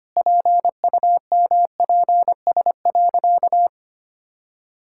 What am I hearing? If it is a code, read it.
Morse code, heard as a keyed tone, about 25 wpm, 700 Hz. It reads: PUMPS.
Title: Pangram Challenge v2 - 25wpm